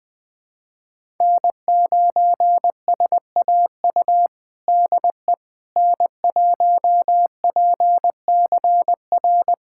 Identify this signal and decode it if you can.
Morse code, heard as a keyed tone, about 20 wpm, 700 Hz. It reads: N9SAU DE N1PCR
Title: USA Callsign DE Callsign - 5 Characters - 20wpm